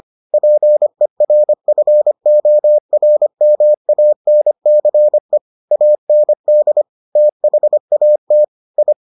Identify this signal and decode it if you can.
Morse code, heard as a keyed tone, about 25 wpm, 600 Hz. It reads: PERFORMANCE AND THAT I